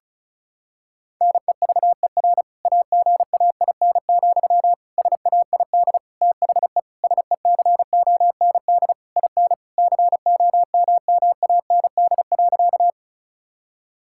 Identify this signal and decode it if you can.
Morse code, heard as a keyed tone, about 36 wpm, 700 Hz. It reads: NEVER AGAIN, SAID THE SECOND IN COMMAND.
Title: Sentences from Top 1600 Words 35wpm